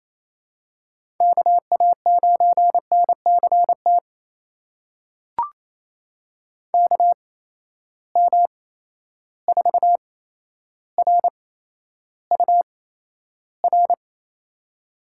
Morse code, heard as a keyed tone, about 28 wpm, spaced as for 12 wpm, 700 Hz. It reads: KA9NCT KM4RUR
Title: USA Call Signs - 6 Characters - 8x Character Spacing - 28wpm